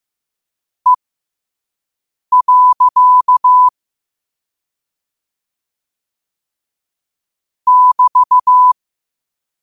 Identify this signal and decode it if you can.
Morse code, heard as a keyed tone, about 15 wpm, spaced as for 5 wpm, 1000 Hz. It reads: E. <BT>
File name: qotdfiles / 5wpm